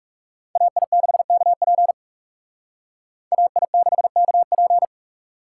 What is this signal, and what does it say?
Morse code, heard as a keyed tone, about 36 wpm, 700 Hz. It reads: AI6KP AI6KP
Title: Speed Racing - USA 2x2 Call Signs 30wpm